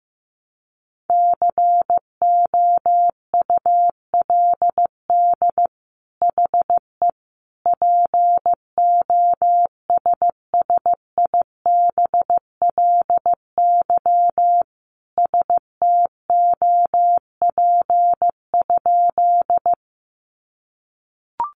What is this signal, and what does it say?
Morse code, heard as a keyed tone, about 14 wpm, 700 Hz. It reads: COULD HE POSSIBLY STOP?
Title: Sentences from Top 1300 Words 15wpm